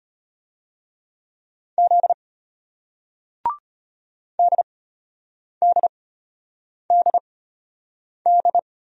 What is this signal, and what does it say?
Morse code, heard as a keyed tone, about 34 wpm, spaced as for 8 wpm, 700 Hz. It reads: Z DDDD